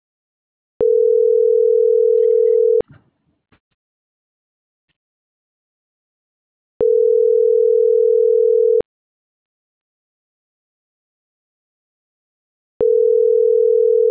Ringback tone (North America), 440 + 480 Hz.